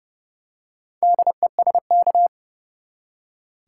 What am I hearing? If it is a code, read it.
Morse code, heard as a keyed tone, about 29 wpm, 700 Hz. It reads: DESK
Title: Four Letter Words - 30wpm